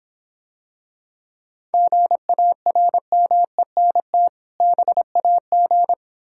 Morse code, heard as a keyed tone, about 25 wpm, 700 Hz. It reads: GARMENT BAG